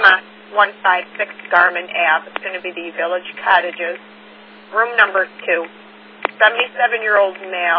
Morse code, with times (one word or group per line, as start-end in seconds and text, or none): none